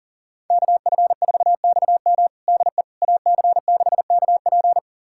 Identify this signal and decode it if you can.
Morse code, heard as a keyed tone, about 40 wpm, 700 Hz. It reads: KF4XM DE AC6KP
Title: USA Callsign DE Callsign - 5 Characters - 40wpm